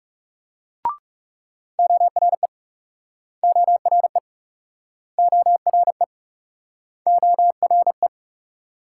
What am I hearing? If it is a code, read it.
Morse code, heard as a keyed tone, about 42 wpm, 700 Hz. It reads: ORE ORE ORE ORE